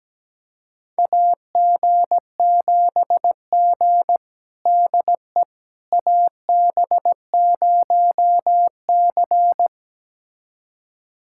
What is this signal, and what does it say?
Morse code, heard as a keyed tone, about 17 wpm, 700 Hz. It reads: AG7G DE AB0C
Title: USA Callsign DE Callsign - 4 Characters - 17wpm